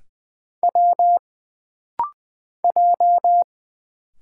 Morse code, heard as a keyed tone, about 20 wpm, 700 Hz. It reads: W J
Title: Single Letters - Rapid-Fire 20wpm